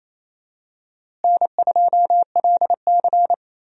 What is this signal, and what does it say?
Morse code, heard as a keyed tone, about 27 wpm, 700 Hz. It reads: N2LC
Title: USA Call Signs - 4 Characters - Repeated 3x - 28wpm